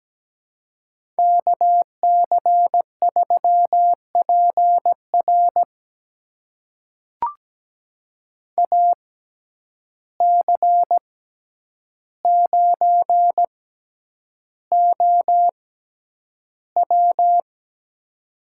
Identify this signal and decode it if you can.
Morse code, heard as a keyed tone, about 17 wpm, spaced as for 9 wpm, 700 Hz. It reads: KC3PR AC9OW
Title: USA Call Signs - 5 Characters - 6x Character Spacing - 17wpm